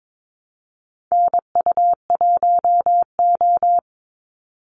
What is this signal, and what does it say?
Morse code, heard as a keyed tone, about 22 wpm, 700 Hz. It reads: NU1O
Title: Call Signs - 22wpm